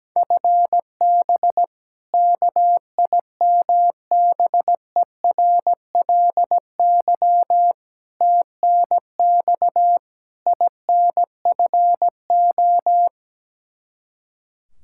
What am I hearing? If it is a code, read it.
Morse code, heard as a keyed tone, about 17 wpm, 700 Hz. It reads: FB KIMBERLY TNX INFO
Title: Practice QSO - Exchange 3 - More Info - 17wpm